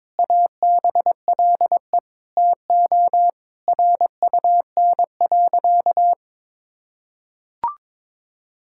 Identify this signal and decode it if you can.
Morse code, heard as a keyed tone, about 22 wpm, 700 Hz. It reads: ABLE TO RUN.
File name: practice / 22wpm / Sentences with 3-Word Phrases - 22wpm